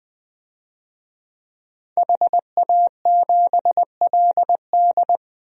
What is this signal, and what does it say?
Morse code, heard as a keyed tone, about 20 wpm, 700 Hz. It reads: HA7LD
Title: DXCC-100 Call Signs - 5 Characters - Repeated 3x - 20wpm